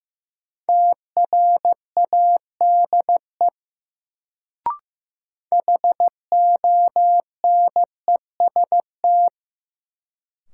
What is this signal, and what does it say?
Morse code, heard as a keyed tone, about 15 wpm, 700 Hz. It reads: TRADE HONEST
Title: Sets of 5 Words Encore - New Word Drill - 15wpm